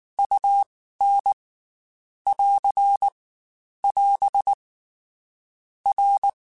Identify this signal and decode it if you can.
Morse code, heard as a keyed tone, about 19 wpm, 800 Hz. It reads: U N <AR> <AS> R